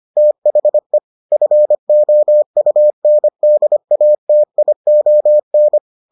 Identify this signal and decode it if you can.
Morse code, heard as a keyed tone, about 25 wpm, 600 Hz. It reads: THE FOUNDATION